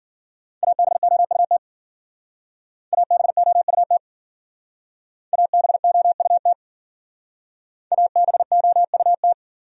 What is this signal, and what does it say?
Morse code, heard as a keyed tone, about 54 wpm, 700 Hz. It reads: ABOUT ABOUT ABOUT ABOUT